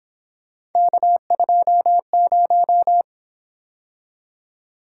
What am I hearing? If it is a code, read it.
Morse code, heard as a keyed tone, about 26 wpm, 700 Hz. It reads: K20
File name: Speed-Racing - Three Letter-Number Combinations - 22wpm